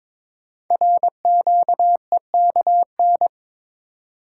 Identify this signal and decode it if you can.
Morse code, heard as a keyed tone, about 22 wpm, 700 Hz. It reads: RQEKN
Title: Code Groups - 5 Letters - 22wpm